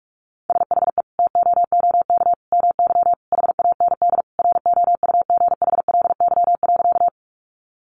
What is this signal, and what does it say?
Morse code, heard as a keyed tone, about 44 wpm, 700 Hz. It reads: SHE TOOK MY HAND ROUGHLY.
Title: Sentences from Top 1500 Words 45wpm